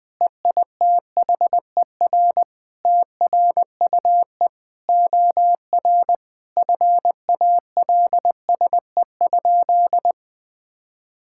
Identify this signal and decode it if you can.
Morse code, heard as a keyed tone, about 20 wpm, 700 Hz. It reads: EITHER TRUE OR FALSE?
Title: Sentences from Top 1600 Words 20wpm